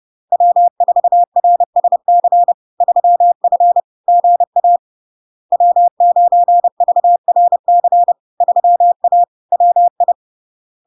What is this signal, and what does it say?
Morse code, heard as a keyed tone, about 30 wpm, 700 Hz. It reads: W4RSC 3F GA W9VRC 3A WI
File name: LARK-FD-2022-Log-in-CW-30wpm-700-Hz